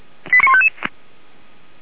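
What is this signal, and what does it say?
Five-tone selective call (ZVEI), heard as quick runs of five tones, tones of about 70 ms. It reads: 89155
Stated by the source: ZVEI five-tone selective call 89155, received by SDR.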